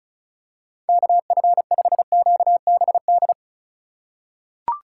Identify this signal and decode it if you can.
Morse code, heard as a keyed tone, about 36 wpm, 700 Hz. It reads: KF5QBD
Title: USA Call Signs - 6 Characters - 4x Character Spacing - 35wpm